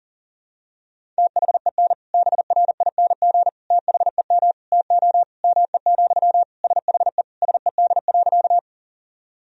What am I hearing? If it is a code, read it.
Morse code, heard as a keyed tone, about 37 wpm, 700 Hz. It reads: THEN BRING THEM TO ME, SHE SED.
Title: Sentences from Top 500 Words and Abbreviations 40wpm